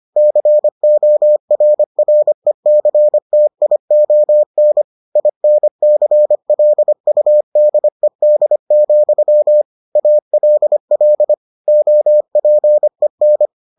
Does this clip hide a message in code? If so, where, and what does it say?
Yes: Morse code from 0.2 to 13.5 s, heard as a keyed tone, about 25 wpm, 600 Hz. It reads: CORRECTION INCLUDED, ALL OPEN